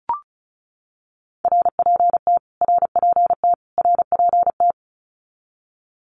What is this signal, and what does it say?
Morse code, heard as a keyed tone, about 34 wpm, 700 Hz. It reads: RPT RPT RPT
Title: Abbreviations Repeated 3x - 35wpm